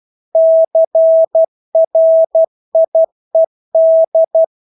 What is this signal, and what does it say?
Morse code, heard as a keyed tone, about 12 wpm, 650 Hz. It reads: CRIED
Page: Engelske ord 12 wpm